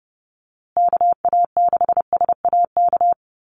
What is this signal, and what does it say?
Morse code, heard as a keyed tone, about 30 wpm, 700 Hz. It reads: KA6SAK
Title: USA Call Signs Repeated 2x 30wpm